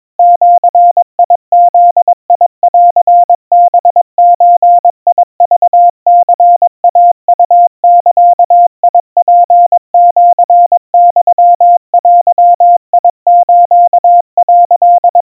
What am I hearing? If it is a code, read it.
Morse code (Japanese Wabun code), heard as a keyed tone, about 22 wpm, 700 Hz. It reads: ジブンハゾクニイウザツシユデス」
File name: 80 cpm 700 Hz Crispy